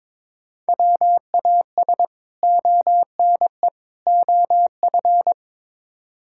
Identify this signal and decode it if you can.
Morse code, heard as a keyed tone, about 22 wpm, 700 Hz. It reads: WAS ONE OF
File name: practice / 22wpm / Build A Phrase - 5-Word Phrases - 22wpm